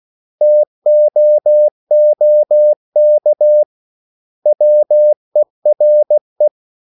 Morse code, heard as a keyed tone, about 16 wpm, 600 Hz. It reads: TOOK WERE
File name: CW / Practice / 500MostCommonEnglishWords16